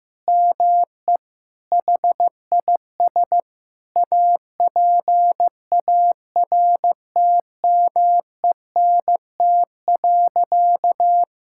Morse code, heard as a keyed tone, about 15 wpm, 700 Hz. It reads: ME HIS APARTMENT.
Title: Sentences from Top 1000 Words - Review Part 6 - 15wpm